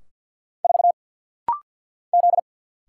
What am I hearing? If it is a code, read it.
Morse code, heard as a keyed tone, about 49 wpm, 700 Hz. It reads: 4 Z
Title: Single Letter-Number Alt - Rapid-Fire 50wpm